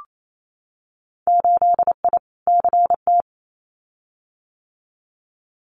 Morse code, heard as a keyed tone, about 28 wpm, 700 Hz. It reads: 8I CT
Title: Winter Field Day Exchanges Encore 28wpm